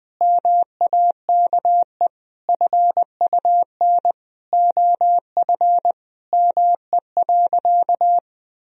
Morse code, heard as a keyed tone, about 20 wpm, 700 Hz. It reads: MAKE FUN OF ME.
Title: Sentences from Top 1000 Words - Review Part 4 - 20wpm